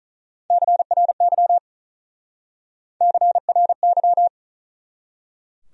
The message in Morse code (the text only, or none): CRY CRY